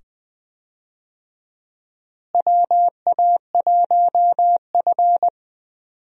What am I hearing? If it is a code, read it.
Morse code, heard as a keyed tone, about 20 wpm, 700 Hz. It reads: WA1F
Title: USA Call Signs - 4 Characters - Repeated 2x - 20wpm